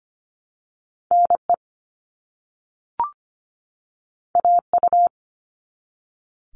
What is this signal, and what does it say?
Morse code, heard as a keyed tone, about 25 wpm, 700 Hz. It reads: NE AU
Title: All Cut Numbers - 2-Digit 25wpm